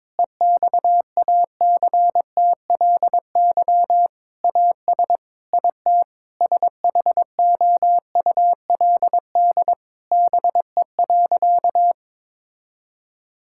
Morse code, heard as a keyed tone, about 22 wpm, 700 Hz. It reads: EXACTLY AS IT SHOULD BE.
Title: Sentences from Top 1000 Words - Review Part 6 - 22wpm